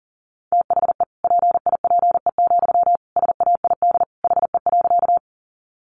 Morse code, heard as a keyed tone, about 40 wpm, 700 Hz. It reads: THE PIPE, SAID HE.
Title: Sentences from Top 1500 Words 40wpm